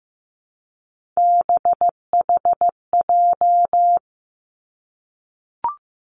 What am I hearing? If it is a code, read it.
Morse code, heard as a keyed tone, about 15 wpm, 700 Hz. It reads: BHJ
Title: Three Challenging Letter-Number Combinations 15wpm